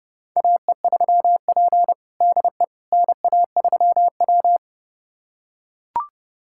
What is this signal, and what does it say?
Morse code, heard as a keyed tone, about 30 wpm, 700 Hz. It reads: AE3P DE NA3W
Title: USA Callsign DE Callsign - 4 Characters - 30wpm